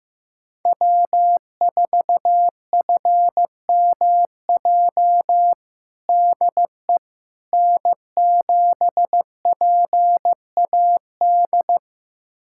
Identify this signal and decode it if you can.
Morse code, heard as a keyed tone, about 15 wpm, 700 Hz. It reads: W4FMJ DE N7PAD
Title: USA Callsign DE Callsign - 5 Characters - 15wpm